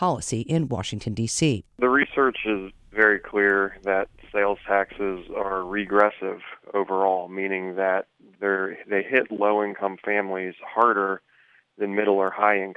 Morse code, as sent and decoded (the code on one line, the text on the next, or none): none